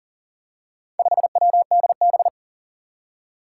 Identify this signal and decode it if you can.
Morse code, heard as a keyed tone, about 44 wpm, 700 Hz. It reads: 5WDB
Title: Four Challenging Letter-Number Combinations 40wpm